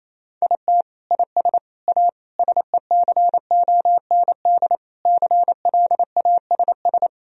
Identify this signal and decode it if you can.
Morse code, heard as a keyed tone, about 28 wpm, 700 Hz. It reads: IT IS A SECOND CLASS
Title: Sentences from Top 1000 Words - Review Part 2 - 28wpm